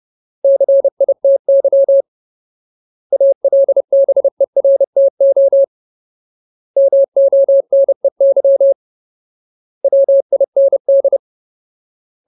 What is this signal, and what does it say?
Morse code, heard as a keyed tone, about 30 wpm, 550 Hz. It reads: CITY ALBERTO MONEY WIND
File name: Ultimate Word List / 30WPM Ultimate